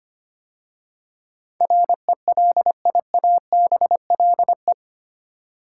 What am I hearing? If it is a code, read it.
Morse code, heard as a keyed tone, about 25 wpm, 700 Hz. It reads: RELIABLE
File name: 8 Letter Words - First 3 Letters Given - 25wpm